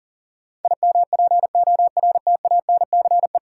Morse code, heard as a keyed tone, about 40 wpm, 700 Hz. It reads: IMPORTANCE
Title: Sentences from Top 1200 Words - New Word Drill Alt - 40wpm